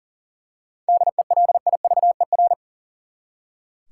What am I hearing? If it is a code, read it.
Morse code, heard as a keyed tone, about 40 wpm, 700 Hz. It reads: DELIVER
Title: Seven Letter Words - 40wpm